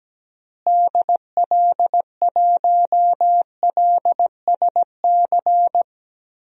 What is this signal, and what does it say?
Morse code, heard as a keyed tone, about 17 wpm, 700 Hz. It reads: DL1LSC
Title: DXCC-100 Call Signs - 6 Characters - Repeated 2x - 17wpm